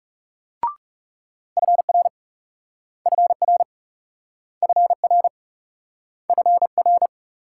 Morse code, heard as a keyed tone, about 39 wpm, 700 Hz. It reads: FR FR FR FR